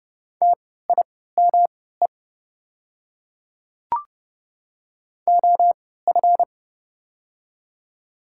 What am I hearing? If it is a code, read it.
Morse code, heard as a keyed tone, about 30 wpm, spaced as for 17 wpm, 700 Hz. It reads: TIME OF
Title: 3x Character Spacing - Top 100 Words - 30wpm